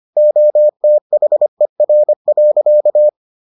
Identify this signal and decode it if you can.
Morse code, heard as a keyed tone, about 25 wpm, 600 Hz. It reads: OTHER.